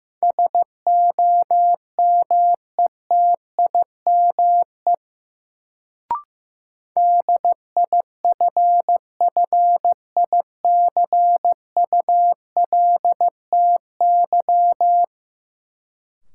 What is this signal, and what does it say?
Morse code, heard as a keyed tone, about 15 wpm, 700 Hz. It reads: SOMETIME DIFFICULTY
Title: Sentences from Top 1500 Words - New Word Drill Alt - 15wpm